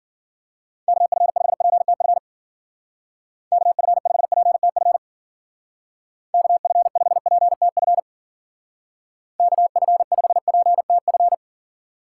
Morse code, heard as a keyed tone, about 51 wpm, 700 Hz. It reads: KF5PTF KF5PTF KF5PTF KF5PTF